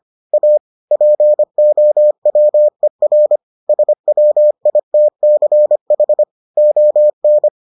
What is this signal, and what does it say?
Morse code, heard as a keyed tone, about 25 wpm, 600 Hz. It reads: A POWER SWITCH ON